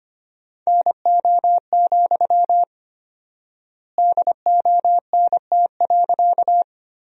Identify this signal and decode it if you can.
Morse code, heard as a keyed tone, about 25 wpm, 700 Hz. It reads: NO, DONT.
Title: 4x Word-Spacing - Sentences from Top 300 Words - 25wpm